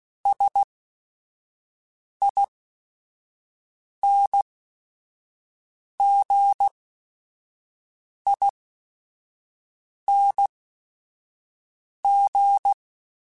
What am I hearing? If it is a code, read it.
Morse code, heard as a keyed tone, about 16 wpm, spaced as for 5 wpm, 800 Hz. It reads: SINGING